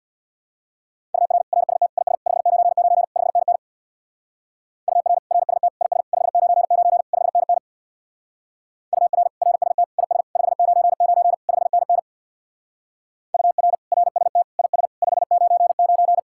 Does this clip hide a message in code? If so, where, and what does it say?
Yes: Morse code from 1.1 to 16.2 s, heard as a keyed tone, about 66 wpm, 700 Hz. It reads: UR RST IS 599 5NN UR RST IS 599 5NN UR RST IS 599 5NN UR RST IS 599